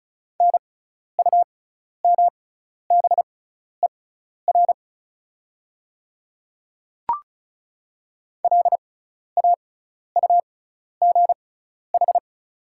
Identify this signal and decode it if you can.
Morse code, heard as a keyed tone, about 34 wpm, spaced as for 12 wpm, 690 Hz. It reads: NUMBER LAUGH
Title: 6x Character Spacing - Top 500 Words - 35wpm